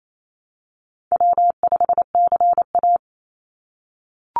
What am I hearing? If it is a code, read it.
Morse code, heard as a keyed tone, about 28 wpm, 700 Hz. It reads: W5CA
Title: USA Call Signs 28wpm